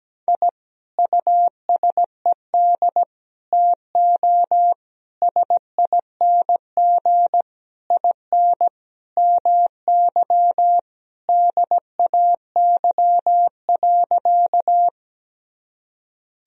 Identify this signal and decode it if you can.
Morse code, heard as a keyed tone, about 17 wpm, 700 Hz. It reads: I USED TO SING IN MY DAY.
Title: Sentences from Top 1000 Words - Review Part 2 - 17wpm